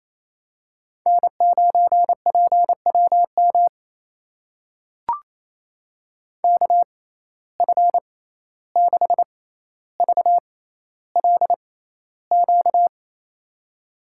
Morse code, heard as a keyed tone, about 28 wpm, spaced as for 14 wpm, 700 Hz. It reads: N9PWM KF6VLQ
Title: USA Call Signs - 6x Character Spacing - 28wpm